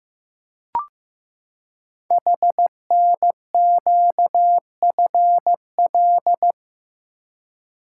Morse code, heard as a keyed tone, about 15 wpm, 700 Hz. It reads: HNQFL
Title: Code Groups - 5 Letters - 15wpm